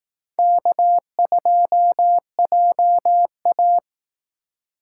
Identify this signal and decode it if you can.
Morse code, heard as a keyed tone, about 18 wpm, 700 Hz. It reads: K2JA
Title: Speed Racing - USA 1x2 Call Signs 15wpm